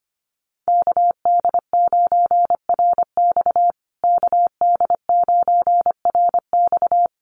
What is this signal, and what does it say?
Morse code, heard as a keyed tone, about 25 wpm, 700 Hz. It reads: KD9RX KD9RX
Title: USA Call Signs Repeated 3x 25wpm